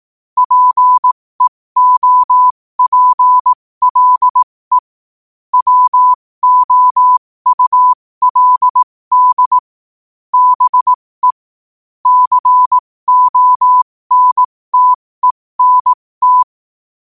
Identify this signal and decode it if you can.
Morse code, heard as a keyed tone, about 18 wpm, spaced as for 15 wpm, 1000 Hz. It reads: PEOPLE WOULD BE CONTENT
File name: qotdfiles / 15wpm